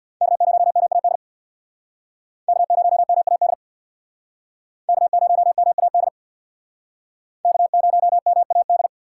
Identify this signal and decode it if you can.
Morse code, heard as a keyed tone, about 63 wpm, 700 Hz. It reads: K0MAD K0MAD K0MAD K0MAD